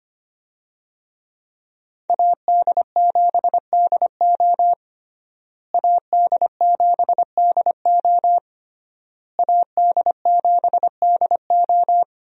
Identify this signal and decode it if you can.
Morse code, heard as a keyed tone, about 25 wpm, 700 Hz. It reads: AD7DO AD7DO AD7DO
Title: USA Call Signs - 5 Characters - Repeated 3x - 25wpm